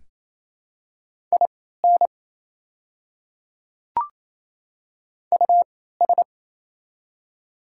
Morse code, heard as a keyed tone, about 28 wpm, spaced as for 16 wpm, 700 Hz. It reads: IN US
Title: 3x Character Spacing - Top 300 Words - 28wpm